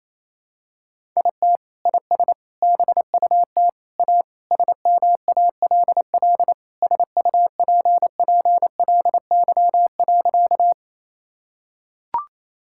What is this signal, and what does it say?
Morse code, heard as a keyed tone, about 28 wpm, 700 Hz. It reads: IT IS BUT A SMALL SUPPLY.
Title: Sentences from Top 1000 Words - Review Part 5 - 28wpm